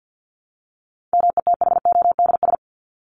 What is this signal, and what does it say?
Morse code, heard as a keyed tone, about 50 wpm, 700 Hz. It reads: METHODS
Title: ES:S Suffix - Top 100 Words with Base Words 50wpm